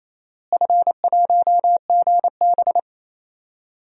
Morse code, heard as a keyed tone, about 28 wpm, 700 Hz. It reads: F1GB
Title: DXCC-100 Call Signs - 4 Characters - Repeated 2x - 28wpm